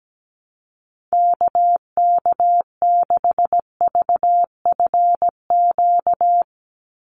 Morse code, heard as a keyed tone, about 17 wpm, 700 Hz. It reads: KK6VFQ
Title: USA Call Signs 17wpm